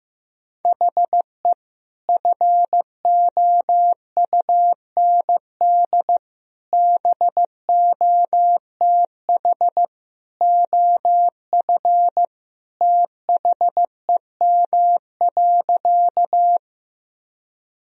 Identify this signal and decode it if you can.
Morse code, heard as a keyed tone, about 15 wpm, 700 Hz. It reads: HE FOUND BOTH OF THEM.
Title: Sentences from Top 1000 Words - Review Part 1 - 15wpm